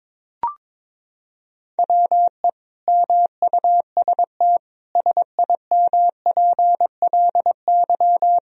Morse code, heard as a keyed tone, about 22 wpm, 700 Hz. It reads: WE MUST SIMPLY
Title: Sentences from Top 1000 Words - Review Part 4 - 22wpm